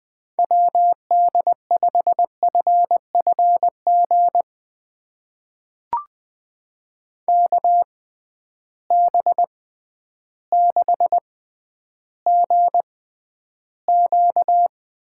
Morse code, heard as a keyed tone, about 20 wpm, spaced as for 11 wpm, 700 Hz. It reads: WD5FFG KB6GQ
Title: USA Call Signs - 6 Characters - 6x Character Spacing - 20wpm